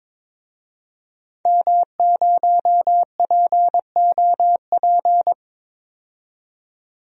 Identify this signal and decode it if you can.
Morse code, heard as a keyed tone, about 22 wpm, 700 Hz. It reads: M0POP
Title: DXCC-100 Call Signs - 5 Characters - 22wpm